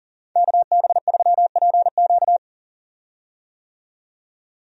Morse code, heard as a keyed tone, about 40 wpm, 700 Hz. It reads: KB3PQ
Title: USA Call Signs - 5 Characters - 40wpm